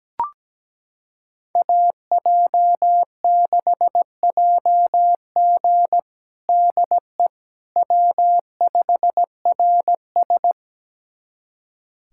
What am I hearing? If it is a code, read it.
Morse code, heard as a keyed tone, about 17 wpm, 700 Hz. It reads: AJ6JG DE W5RS
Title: USA Callsign DE Callsign 17wpm